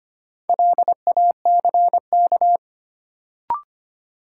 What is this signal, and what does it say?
Morse code, heard as a keyed tone, about 25 wpm, 700 Hz. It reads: LACK
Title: Sets of 3 Words - New Word Drill - 25wpm